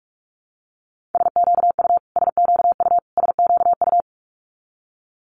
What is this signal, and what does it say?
Morse code, heard as a keyed tone, about 46 wpm, 700 Hz. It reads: SQU SQU SQU
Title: Multi-letter Phonemes Repeated 3x 45wpm